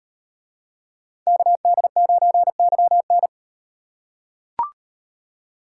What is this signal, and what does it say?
Morse code, heard as a keyed tone, about 38 wpm, 700 Hz. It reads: KD9YN